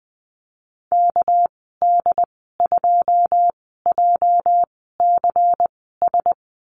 Morse code, heard as a keyed tone, about 20 wpm, spaced as for 14 wpm, 700 Hz. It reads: KD2JCS
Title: 231 Lesson 37g - Letter x - Call Signs 20wpm